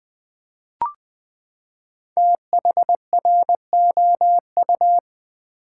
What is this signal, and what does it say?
Morse code, heard as a keyed tone, about 20 wpm, 700 Hz. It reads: THROU
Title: Build A Word - 5 Letters At A Time - 10-Letter Words - 20wpm